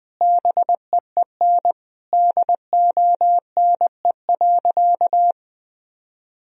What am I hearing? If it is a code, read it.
Morse code, heard as a keyed tone, about 20 wpm, 700 Hz. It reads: BEEN DONE.
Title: Sentences from Top 1000 Words - Review Part 1 - 20wpm